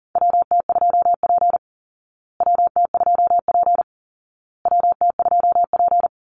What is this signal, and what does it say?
Morse code, heard as a keyed tone, about 40 wpm, 700 Hz. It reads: WT2P WT2P WT2P
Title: Call Signs Repeated 2x - 40wpm